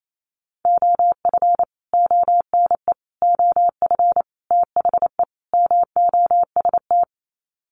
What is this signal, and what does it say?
Morse code, heard as a keyed tone, about 28 wpm, 700 Hz. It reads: OF ONE OF THE MOST
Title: Sets of 5 Words Encore2 28wpm